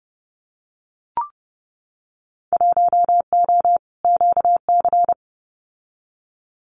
Morse code, heard as a keyed tone, about 30 wpm, 700 Hz. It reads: 1O QC
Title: Winter Field Day Exchanges 30wpm